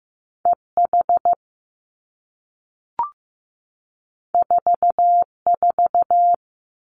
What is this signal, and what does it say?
Morse code, heard as a keyed tone, about 15 wpm, 700 Hz. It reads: E H 4 4
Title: Two Challenging Letter-Number Combinations 15wpm